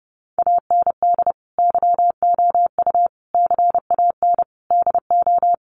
Morse code, heard as a keyed tone, about 28 wpm, 700 Hz. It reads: AND YOU CAN DO
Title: Sentences from Top 500 Words 30wpm